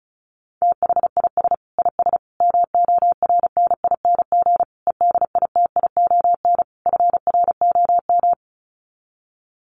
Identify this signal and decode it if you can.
Morse code, heard as a keyed tone, about 34 wpm, 700 Hz. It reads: THIS IS MORNING EDITION FROM
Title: Sets of 5 Words Encore2 35wpm